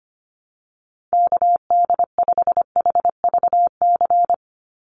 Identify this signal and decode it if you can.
Morse code, heard as a keyed tone, about 25 wpm, 700 Hz. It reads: KD5HVC
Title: USA Call Signs 25wpm